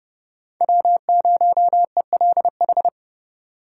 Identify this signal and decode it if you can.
Morse code, heard as a keyed tone, about 30 wpm, 700 Hz. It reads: W0ELH
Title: USA Call Signs - 5 Characters - 2x Character Spacing - 30wpm